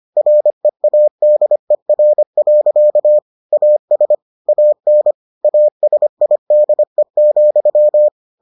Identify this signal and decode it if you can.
Morse code, heard as a keyed tone, about 25 wpm, 600 Hz. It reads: READER. AS AN ASIDE,